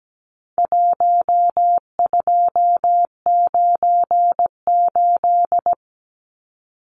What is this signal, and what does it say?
Morse code, heard as a keyed tone, about 17 wpm, 700 Hz. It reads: 1298
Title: Numbers - 4-Digit 17wpm